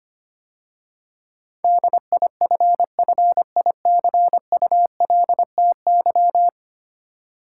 Morse code, heard as a keyed tone, about 25 wpm, 700 Hz. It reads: DIFFICULTY
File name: 10 Letter Words - First 3 Letters Given - 25wpm